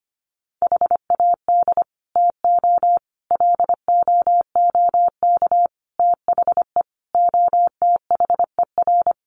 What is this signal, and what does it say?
Morse code, heard as a keyed tone, about 25 wpm, 700 Hz. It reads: HAD TO LOOK THE OTHER
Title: Sentences from Top 100 Words 25wpm